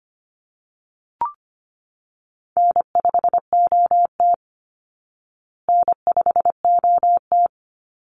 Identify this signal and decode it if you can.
Morse code, heard as a keyed tone, about 25 wpm, 700 Hz. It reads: N5OT N5OT
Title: Easy Call Signs Repeated 3x - 25wpm